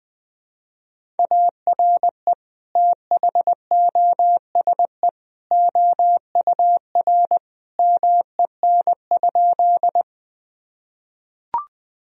Morse code, heard as a keyed tone, about 20 wpm, 700 Hz. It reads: ARE THOSE OUR MEN?
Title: Sentences from Top 1000 Words - Review Part 1 - 20wpm